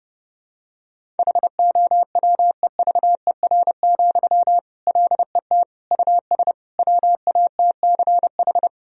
Morse code, heard as a keyed tone, about 30 wpm, 700 Hz. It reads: HOWEVER, LET US WATCH
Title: Sentences from Top 1000 Words - Review Part 1 - 30wpm